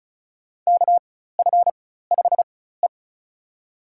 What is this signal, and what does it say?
Morse code, heard as a keyed tone, about 34 wpm, spaced as for 16 wpm, 700 Hz. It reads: KF5E
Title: USA Call Signs - 4 Characters - 4x Character Spacing - 35wpm